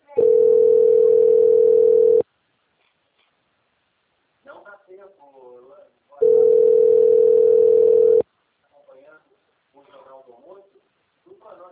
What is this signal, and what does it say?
Ringback tone (North America), 440 + 480 Hz, with speech.
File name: out- - - - - .37654